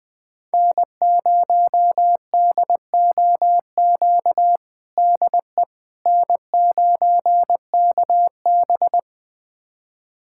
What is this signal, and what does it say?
Morse code, heard as a keyed tone, about 20 wpm, 700 Hz. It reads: N0DOQ DE N9KB
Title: USA Callsign DE Callsign 20wpm